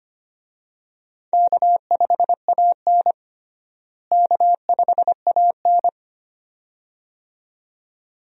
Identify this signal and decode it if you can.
Morse code, heard as a keyed tone, about 25 wpm, 700 Hz. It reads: K5AN K5AN
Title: USA Call Signs - 4 Characters - Repeated 2x - 25wpm